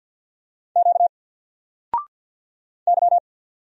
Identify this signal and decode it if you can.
Morse code, heard as a keyed tone, about 50 wpm, 700 Hz. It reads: Q Y
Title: Single Letter-Number 50wpm